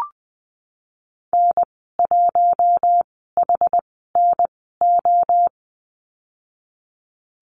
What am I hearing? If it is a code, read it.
Morse code, heard as a keyed tone, about 20 wpm, spaced as for 14 wpm, 700 Hz. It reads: N1HNO
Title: 088 Lesson 15g - Number 5 - Call Signs 20wpm